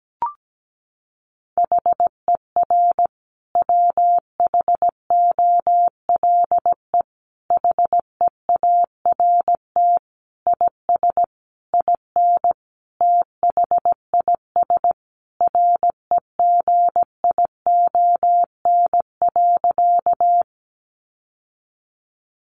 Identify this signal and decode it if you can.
Morse code, heard as a keyed tone, about 17 wpm, 700 Hz. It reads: HER WHOLE HEART IS IN THIS REGION.
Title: Sentences from Top 1500 Words 17wpm